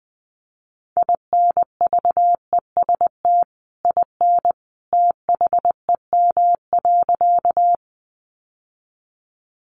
Morse code, heard as a keyed tone, about 19 wpm, 700 Hz. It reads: INVEST IN THEM.
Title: Sentences from Top 1100 Words 20wpm